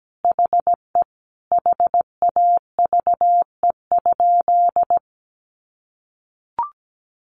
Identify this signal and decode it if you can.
Morse code, heard as a keyed tone, about 17 wpm, 700 Hz. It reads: HE HAVE?
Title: Sentences from Top 1000 Words 17wpm